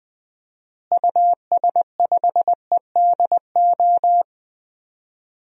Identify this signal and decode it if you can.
Morse code, heard as a keyed tone, about 20 wpm, 700 Hz. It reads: US5EDO
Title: DXCC-100 Call Signs - 6 Characters - Repeated 2x - 20wpm